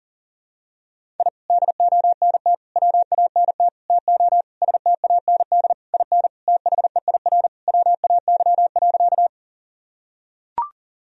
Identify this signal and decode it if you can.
Morse code, heard as a keyed tone, about 40 wpm, 700 Hz. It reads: I DONT WANT TO STAND IN THEIR WAY.
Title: Sentences from Top 1000 Words - Review Part 1 - 40wpm